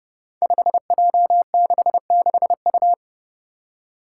Morse code, heard as a keyed tone, about 28 wpm, 700 Hz. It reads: 5J66U
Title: Five Challenging Letter-Number Combinations 30wpm